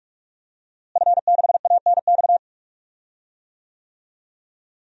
Morse code, heard as a keyed tone, about 46 wpm, 700 Hz. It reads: F6ANX
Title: DXCC-100 Call Signs - 5 Characters - Repeated 2x - 45wpm